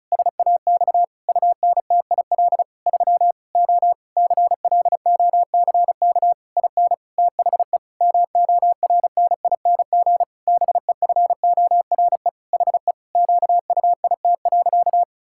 Morse code, heard as a keyed tone, about 36 wpm, 700 Hz. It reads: SAX UNTIL 3 O CLOCK IN THE MORNING BEFORE HE QUIT.
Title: Pangram Challenge v2 - 35wpm